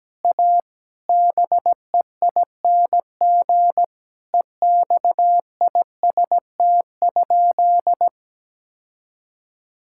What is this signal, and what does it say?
Morse code, heard as a keyed tone, about 17 wpm, 700 Hz. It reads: A BEING EXIST?
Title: Sentences from Top 1000 Words - Review Part 5 - 17wpm